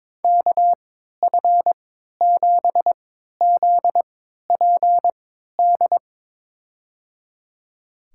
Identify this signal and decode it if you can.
Morse code, heard as a keyed tone, about 22 wpm, spaced as for 13 wpm, 700 Hz. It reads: KF7ZPD
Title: USA Call Signs - 3x Character Spacing - 22wpm